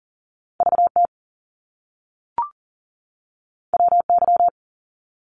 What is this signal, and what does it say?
Morse code, heard as a keyed tone, about 38 wpm, 700 Hz. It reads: VT WY
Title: US States 40wpm